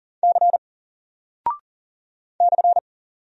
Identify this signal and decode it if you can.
Morse code, heard as a keyed tone, about 39 wpm, 700 Hz. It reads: C /
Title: Single Character (Letter:Number:Punctuation:Prosign) 40wpm